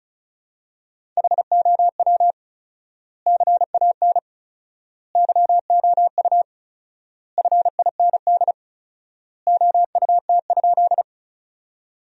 Morse code, heard as a keyed tone, about 36 wpm, 700 Hz. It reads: HOW CAN YOU FIND OUT?
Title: 4x Word-Spacing - Sentences from Top 200 Words - 35wpm